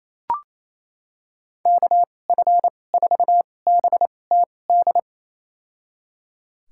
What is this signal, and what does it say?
Morse code, heard as a keyed tone, about 28 wpm, spaced as for 20 wpm, 700 Hz. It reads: KF4BTD
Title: USA Call Signs - 2x Character Spacing - 28wpm